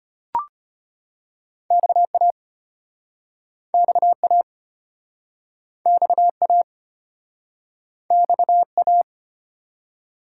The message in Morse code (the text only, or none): XA XA XA XA